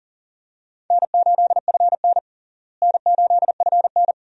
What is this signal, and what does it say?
Morse code, heard as a keyed tone, about 40 wpm, 700 Hz. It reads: N8FN N8FN
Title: USA Call Signs - 4 Characters - Repeated 2x - 40wpm